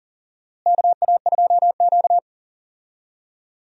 Morse code, heard as a keyed tone, about 40 wpm, 700 Hz. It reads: KA2Q
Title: USA Call Signs - 4 Characters - Repeated 2x - 40wpm